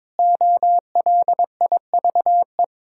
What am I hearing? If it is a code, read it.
Morse code, heard as a keyed tone, about 22 wpm, 700 Hz. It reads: OLIVE
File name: Difficult Words with Hints - 22wpm